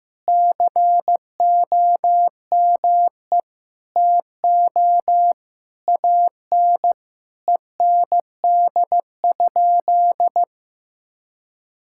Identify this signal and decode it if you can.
Morse code, heard as a keyed tone, about 15 wpm, 700 Hz. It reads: COME TO AN END?
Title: Sentences from Top 1000 Words - Review Part 5 - 15wpm